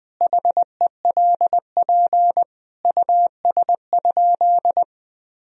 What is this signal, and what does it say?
Morse code, heard as a keyed tone, about 20 wpm, 700 Hz. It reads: HELP US?
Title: Sentences from Top 1000 Words - Review Part 3 - 20wpm